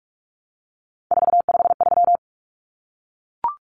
Morse code, heard as a keyed tone, about 44 wpm, 700 Hz. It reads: 453
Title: Common Cut Numbers - 3-Digit 45wpm